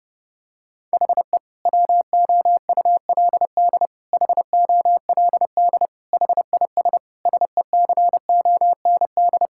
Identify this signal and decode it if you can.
Morse code, heard as a keyed tone, about 32 wpm, 700 Hz. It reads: HE WOULD HOLD HIS SECOND
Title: Sentences from Top 1000 Words - Review Part 6 - 30wpm